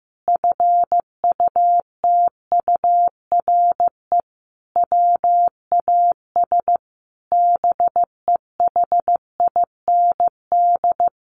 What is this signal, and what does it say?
Morse code, heard as a keyed tone, about 15 wpm, 700 Hz. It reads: FUTURE WAS BEHIND
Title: Sentences from Top 700 Words 15wpm